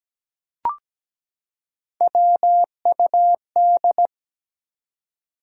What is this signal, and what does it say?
Morse code, heard as a keyed tone, about 17 wpm, 700 Hz. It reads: WUD
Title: Top 180 QSO Elements 17wpm